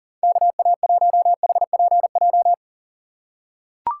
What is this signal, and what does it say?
Morse code, heard as a keyed tone, about 40 wpm, 700 Hz. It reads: KA1HPJ
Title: USA Call Signs - 6 Characters - 4x Character Spacing - 40wpm